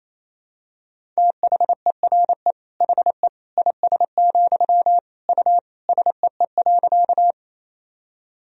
Morse code, heard as a keyed tone, about 28 wpm, 700 Hz. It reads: THERE HE IS, U SEE.
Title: Sentences from Top 100 Words and Abbreviations 28wpm